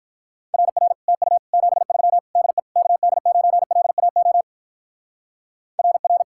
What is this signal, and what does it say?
Morse code, heard as a keyed tone, about 45 wpm, 700 Hz. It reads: RR TU 73 DE KD9LAO RR